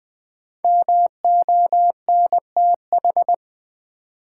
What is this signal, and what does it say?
Morse code, heard as a keyed tone, about 21 wpm, 700 Hz. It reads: MONTH
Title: Sentences from Top 800 Words - New Word Drill - 20wpm